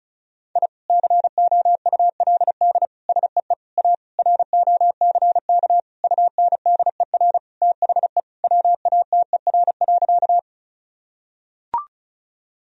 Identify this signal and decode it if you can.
Morse code, heard as a keyed tone, about 36 wpm, 700 Hz. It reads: I COULD SEE A ROCK UNDER THE WATER.
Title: Sentences from Top 1000 Words - Review Part 2 - 35wpm